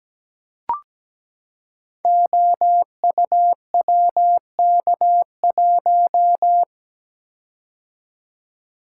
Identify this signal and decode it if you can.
Morse code, heard as a keyed tone, about 17 wpm, 700 Hz. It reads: OUWK1
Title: Code Groups - 5 Letter:Numbers - 17wpm